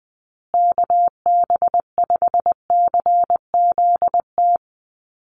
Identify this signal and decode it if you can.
Morse code, heard as a keyed tone, about 20 wpm, 700 Hz. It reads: KB5CZT
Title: USA Call Signs Repeated 2x 20wpm